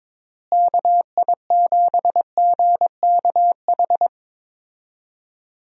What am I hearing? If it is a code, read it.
Morse code, heard as a keyed tone, about 22 wpm, 700 Hz. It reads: KI7GKH
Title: USA Call Signs - 6 Characters - Repeated 2x - 22wpm